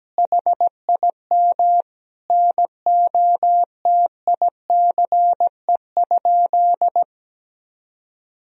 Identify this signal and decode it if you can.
Morse code, heard as a keyed tone, about 17 wpm, 700 Hz. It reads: HIM NOTICE?
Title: Sentences from Top 1000 Words - Review Part 2 - 17wpm